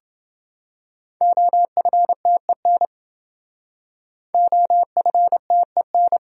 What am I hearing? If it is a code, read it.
Morse code, heard as a keyed tone, about 28 wpm, 700 Hz. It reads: OFTEN OFTEN